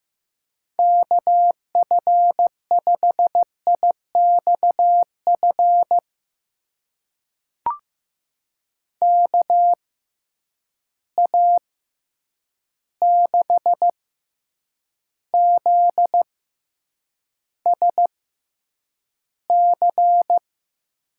Morse code, heard as a keyed tone, about 15 wpm, spaced as for 8 wpm, 700 Hz. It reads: KF5IXF KA6ZSC
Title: USA Call Signs - 6 Characters - 6x Character Spacing - 15wpm